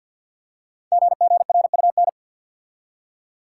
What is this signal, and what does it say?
Morse code, heard as a keyed tone, about 53 wpm, 700 Hz. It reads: GGRUN
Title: Code Groups - 5 Letters - 50wpm